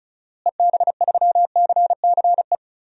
Morse code, heard as a keyed tone, about 36 wpm, 700 Hz. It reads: EB3CCE
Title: DXCC-100 Call Signs - 6 Characters - Repeated 3x - 35wpm